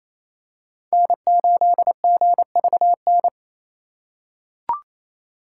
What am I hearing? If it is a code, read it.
Morse code, heard as a keyed tone, about 28 wpm, 700 Hz. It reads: N8GVN
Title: USA Call Signs - 5 Characters - 2x Character Spacing - 28wpm